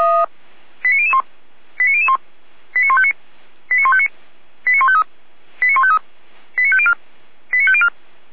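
Five-tone selective call (ZVEI), heard as quick runs of five tones, tones of about 70 ms. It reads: 89001 89001 89159 89159 89153 89153 89504 89504
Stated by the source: ZVEI five-tone selective call 89099 received on an SDR, from the feuerwehr-orlamuende.org (fire brigade Orlamünde) recording server.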